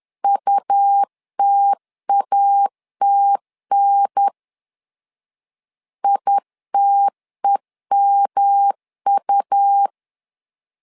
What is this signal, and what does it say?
Morse code, heard as a keyed tone, about 11 wpm, 800 Hz. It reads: UTATN ITEMU
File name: LEZIONI / 3 MWRO / 3 M 10wpm 1828